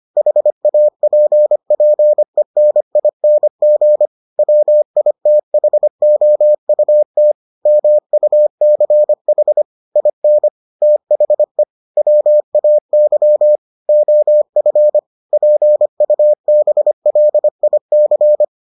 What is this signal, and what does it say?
Morse code, heard as a keyed tone, about 25 wpm, 600 Hz. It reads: HAPPENING WITHOUT MUCH IN THE WAY OF PUBLIC